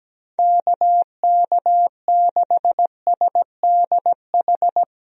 Morse code, heard as a keyed tone, about 17 wpm, 700 Hz. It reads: KK6SDH
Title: USA Call Signs - 6 Characters - 17wpm